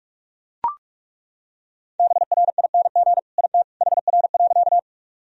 Morse code, heard as a keyed tone, about 41 wpm, 690 Hz. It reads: BRING IT HR.
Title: Sentences from Top 500 Words and Abbreviations 45wpm